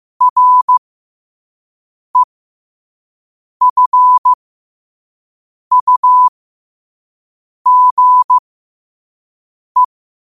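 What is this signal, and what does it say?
Morse code, heard as a keyed tone, about 15 wpm, spaced as for 5 wpm, 1000 Hz. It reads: REFUGE
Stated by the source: Morse quote-of-the-day practice at 5 wpm.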